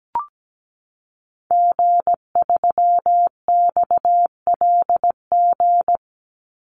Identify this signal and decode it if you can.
Morse code, heard as a keyed tone, about 17 wpm, 700 Hz. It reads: G3XLG
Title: Call Signs - 17wpm